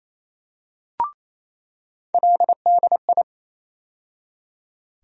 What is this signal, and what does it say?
Morse code, heard as a keyed tone, about 28 wpm, 700 Hz. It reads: LDI
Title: USA Call Sign Postfixes - 3 Letters 28wpm